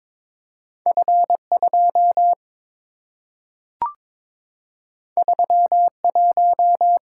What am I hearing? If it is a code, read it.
Morse code, heard as a keyed tone, about 22 wpm, 700 Hz. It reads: F2 31
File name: practice / 22wpm / Two Letter-Number Combinations - 22wpm